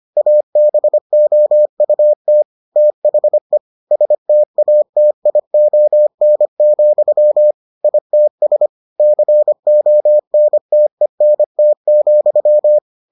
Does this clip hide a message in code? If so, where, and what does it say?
Yes: Morse code from 0.2 to 12.8 s, heard as a keyed tone, about 25 wpm, 600 Hz. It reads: ABOUT THE STATION, ITS CONTENT,